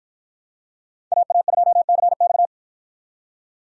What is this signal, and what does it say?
Morse code, heard as a keyed tone, about 49 wpm, 700 Hz. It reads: AA2CX